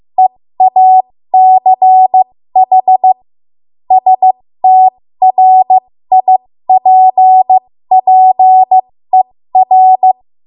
Morse code, heard as a keyed tone, about 15 wpm, spaced as for 13 wpm, 750 Hz. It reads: EACH STRIPPER
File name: Morse / 13WPM